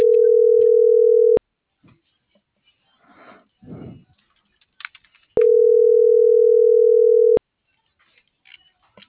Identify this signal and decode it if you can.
Ringback tone (North America), 440 + 480 Hz.